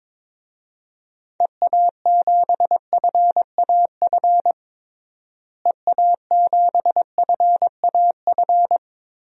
Morse code, heard as a keyed tone, about 22 wpm, 700 Hz. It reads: EA7FAF EA7FAF
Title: DXCC-100 Call Signs - 6 Characters - Repeated 3x - 22wpm